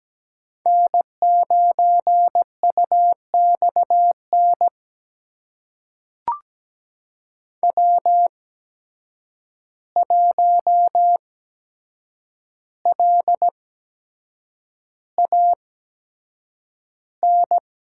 Morse code, heard as a keyed tone, about 17 wpm, spaced as for 7 wpm, 700 Hz. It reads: N9UXN W1LAN
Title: USA Call Signs - 5 Characters - 8x Character Spacing - 17wpm